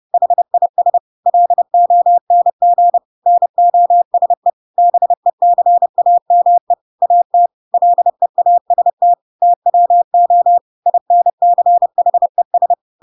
Morse code, heard as a keyed tone, about 30 wpm, 700 Hz. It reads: HIS LONG NOSE BECAME AT LEAST TWO INCHES
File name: Pinocchio / Pinocchio ch5 30 30 ik1hge